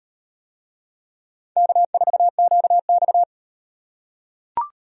Morse code, heard as a keyed tone, about 38 wpm, 700 Hz. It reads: K4QX